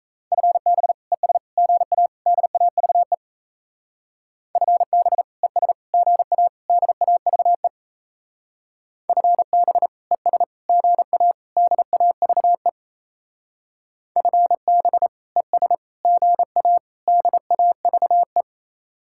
Morse code, heard as a keyed tone, about 33 wpm, 700 Hz. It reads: FB ES GA DAVE FB ES GA DAVE FB ES GA DAVE FB ES GA DAVE
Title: Speed-Racing Common QSO Phrases Encore - 28wpm